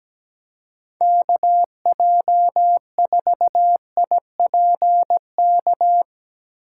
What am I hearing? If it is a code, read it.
Morse code, heard as a keyed tone, about 17 wpm, 700 Hz. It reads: KJ4IPK
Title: USA Call Signs - 6 Characters - 4x Character Spacing - 17wpm